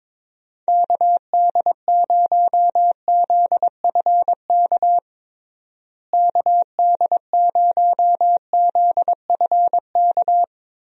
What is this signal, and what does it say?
Morse code, heard as a keyed tone, about 22 wpm, 700 Hz. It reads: KD0ZFK KD0ZFK
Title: USA Call Signs - 6 Characters - Repeated 3x - 22wpm